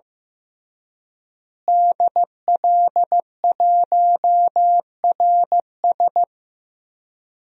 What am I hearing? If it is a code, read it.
Morse code, heard as a keyed tone, about 15 wpm, 700 Hz. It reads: DL1RS
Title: DXCC-100 Call Signs - 5 Characters - Repeated 3x - 15wpm